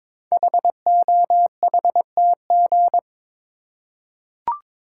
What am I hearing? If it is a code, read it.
Morse code, heard as a keyed tone, about 21 wpm, 700 Hz. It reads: HOHTG
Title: Code Groups - 5 Letters - 22wpm